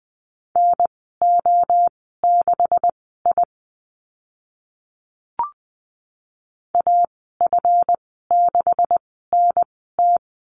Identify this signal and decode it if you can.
Morse code, heard as a keyed tone, about 20 wpm, spaced as for 14 wpm, 700 Hz. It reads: NO6I AF6NT
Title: 147 Lesson 24g - Letter y - Call Signs 20wpm